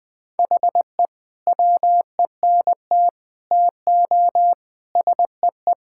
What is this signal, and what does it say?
Morse code, heard as a keyed tone, about 20 wpm, 700 Hz. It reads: HE WENT TO SEE
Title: Sentences with 8-Word Phrases - 20wpm